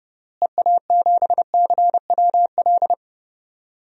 Morse code, heard as a keyed tone, about 30 wpm, 700 Hz. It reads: EA7CWL
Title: DXCC-100 Call Signs - 6 Characters - Repeated 3x - 30wpm